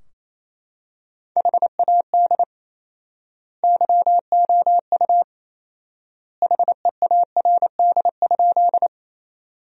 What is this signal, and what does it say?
Morse code, heard as a keyed tone, about 27 wpm, 700 Hz. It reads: HAD YOU HEARD?
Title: 4x Word-Spacing - Sentences from Top 400 Words - 28wpm